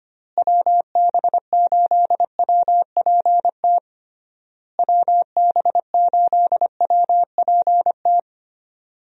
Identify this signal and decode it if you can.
Morse code, heard as a keyed tone, about 25 wpm, 700 Hz. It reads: WB8WPT WB8WPT
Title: USA Call Signs - 6 Characters - Repeated 3x - 25wpm